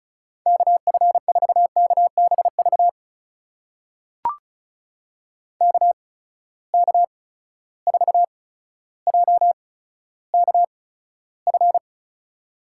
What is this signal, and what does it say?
Morse code, heard as a keyed tone, about 36 wpm, spaced as for 15 wpm, 700 Hz. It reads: KF4KBV KK4JKF
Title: USA Call Signs - 6 Characters - 8x Character Spacing - 35wpm